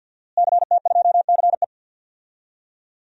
Morse code, heard as a keyed tone, about 49 wpm, 700 Hz. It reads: CT2CE
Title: DXCC-100 Call Signs - 5 Characters - Repeated 3x - 50wpm